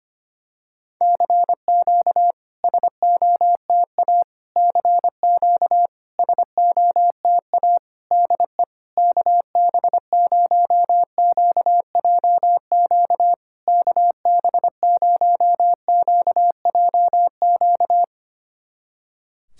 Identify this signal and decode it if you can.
Morse code, heard as a keyed tone, about 25 wpm, 700 Hz. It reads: CQ SOTA CQ SOTA DE KB0QJQ KB0QJQ
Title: SOTA QSO Practice 25wpm